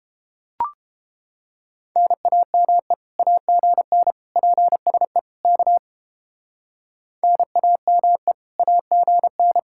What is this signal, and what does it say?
Morse code, heard as a keyed tone, about 29 wpm, 700 Hz. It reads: NAME AGN PSE K NAME AGN